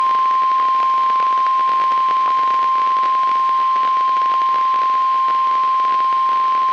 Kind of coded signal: Emergency Alert System (SAME) tones

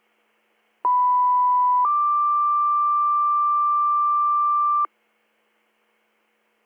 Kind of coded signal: two-tone page

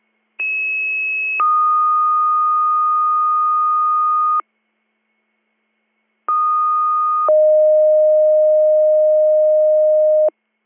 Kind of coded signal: two-tone page

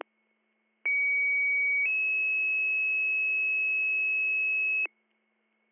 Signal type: two-tone page

